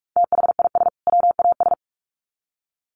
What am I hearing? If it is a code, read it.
Morse code, heard as a keyed tone, about 44 wpm, 700 Hz. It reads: THIS WAS
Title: Sets of 2 Words 45wpm